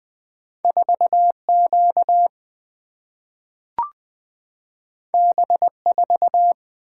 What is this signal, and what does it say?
Morse code, heard as a keyed tone, about 20 wpm, 700 Hz. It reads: 4Q B4
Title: Single-Letter Single-Number 20wpm